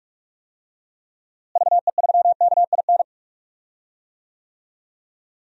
Morse code, heard as a keyed tone, about 46 wpm, 700 Hz. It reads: VE3KIN
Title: DXCC-100 Call Signs - 6 Characters - Repeated 3x - 45wpm